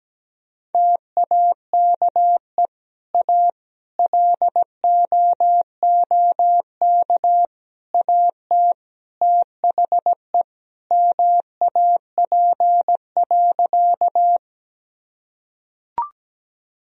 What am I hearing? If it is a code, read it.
Morse code, heard as a keyed tone, about 17 wpm, 700 Hz. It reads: TAKE A LOOK AT THE MAP.
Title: Sentences from Top 1000 Words - Review Part 2 - 17wpm